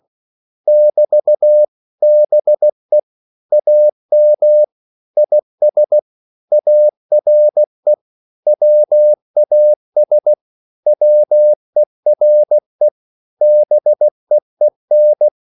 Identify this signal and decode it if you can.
Morse code, heard as a keyed tone, about 16 wpm, 600 Hz. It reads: <BT> BE AM IS ARE WAS WERE BEEN
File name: CW / Practice / 500MostCommonEnglishWords16